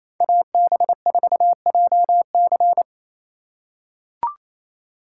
Morse code, heard as a keyed tone, about 28 wpm, 700 Hz. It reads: AB4JC
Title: USA Call Signs - 5 Characters - 2x Character Spacing - 28wpm